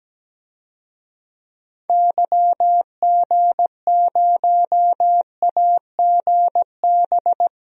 Morse code, heard as a keyed tone, about 17 wpm, 700 Hz. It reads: YG0AGB
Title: DXCC-100 Call Signs - 6 Characters - 17wpm